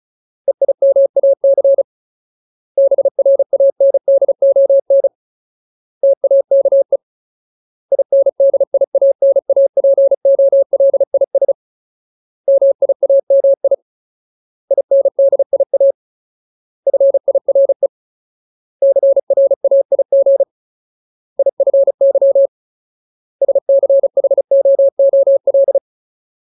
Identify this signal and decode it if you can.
Morse code, heard as a keyed tone, about 34 wpm, 550 Hz. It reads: EIMAC BRANDON TAKE INDIANAPOLIS MIAMI INDIA FIRE CRAIG IFY SCHOOL
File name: Ultimate Word List / 35WPM Ultimate